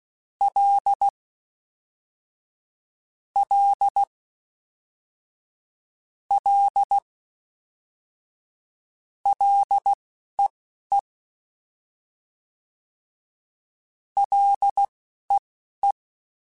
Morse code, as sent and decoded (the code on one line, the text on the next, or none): .-.. / .-.. / .-.. / .-.. . / .-.. .
L L L LEE LEE